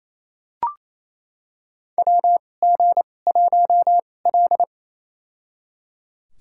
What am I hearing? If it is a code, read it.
Morse code, heard as a keyed tone, about 28 wpm, spaced as for 20 wpm, 700 Hz. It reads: WG1L
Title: USA Call Signs - 4 Characters - 2x Character Spacing - 28wpm